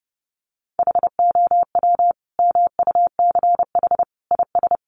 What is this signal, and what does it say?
Morse code, heard as a keyed tone, about 30 wpm, 700 Hz. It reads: HOW MUCH IS
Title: Sentences from Top 1200 Words 30wpm